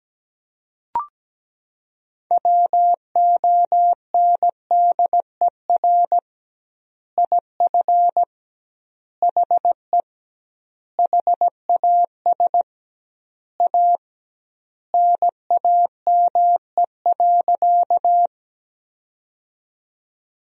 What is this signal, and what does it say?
Morse code, heard as a keyed tone, about 17 wpm, 700 Hz. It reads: WONDER IF HE HAS A NAME.
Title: 2x Word-Spacing - Sentences from Top 500 Words - 17wpm